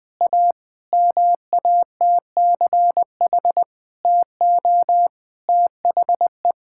Morse code, heard as a keyed tone, about 20 wpm, 700 Hz. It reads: A MATCH TO THE
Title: Sentences from Top 1600 Words 20wpm